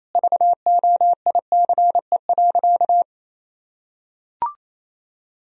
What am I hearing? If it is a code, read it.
Morse code, heard as a keyed tone, about 28 wpm, 700 Hz. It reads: VOICE.
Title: Sentences from Top 1000 Words - Review Part 1 - 28wpm